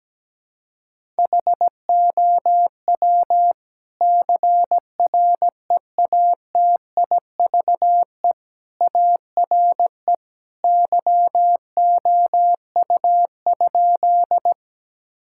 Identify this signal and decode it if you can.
Morse code, heard as a keyed tone, about 17 wpm, 700 Hz. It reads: HOW CREATIVE ARE YOU?
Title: Sentences from Top 1000 Words - Review Part 5 - 17wpm